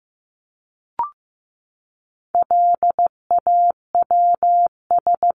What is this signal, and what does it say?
Morse code, heard as a keyed tone, about 15 wpm, 700 Hz. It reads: LAWS
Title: ES:S Suffix - Top 100 Words with Base Words 15wpm